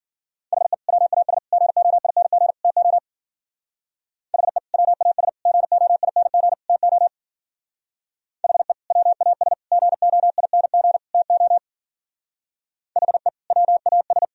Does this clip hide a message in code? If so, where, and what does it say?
Yes: Morse code from 0.5 to 14.2 s, heard as a keyed tone, about 51 wpm, 700 Hz. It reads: HE WAS GOING TO HE WAS GOING TO HE WAS GOING TO HE WAS